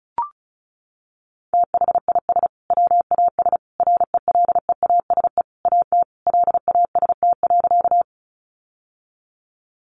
Morse code, heard as a keyed tone, about 34 wpm, 700 Hz. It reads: THIS WAS RELEASE AT LAST.
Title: Sentences from Top 1100 Words 35wpm